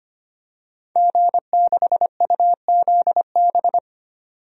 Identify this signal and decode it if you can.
Morse code, heard as a keyed tone, about 25 wpm, 700 Hz. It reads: G6UZB
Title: DXCC-100 Call Signs - 5 Characters - 25wpm